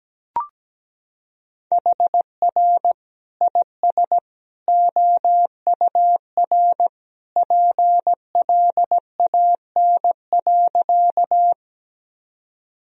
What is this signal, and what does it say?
Morse code, heard as a keyed tone, about 17 wpm, 700 Hz. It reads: HR IS OUR PLAN.
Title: Sentences from Top 500 Words and Abbreviations 17wpm